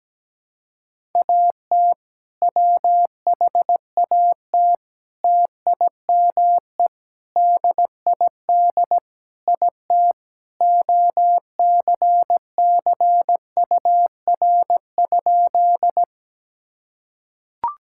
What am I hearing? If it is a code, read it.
Morse code, heard as a keyed tone, about 17 wpm, 700 Hz. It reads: AT WHAT TIME DID IT OCCUR?
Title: Sentences from Top 1000 Words - Review Part 6 - 17wpm